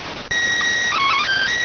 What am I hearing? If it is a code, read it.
SSTV (slow-scan television), PD 290.